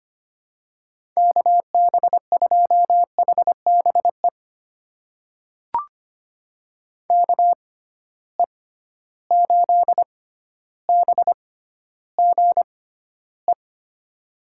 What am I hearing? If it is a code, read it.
Morse code, heard as a keyed tone, about 25 wpm, spaced as for 13 wpm, 700 Hz. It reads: KB2HBE KE8BGE
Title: USA Call Signs - 6 Characters - 6x Character Spacing - 25wpm